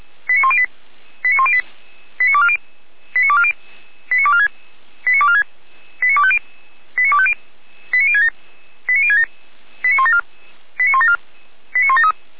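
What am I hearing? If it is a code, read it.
Five-tone selective call (ZVEI), heard as quick runs of five tones, tones of about 70 ms. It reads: 89108 89240 89240 89246 89246 89250 89250 89067 89067 89174 89174 89173
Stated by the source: ZVEI five-tone call 89337 received by SDR, from feuerwehr-orlamuende.org (fire brigade Orlamünde).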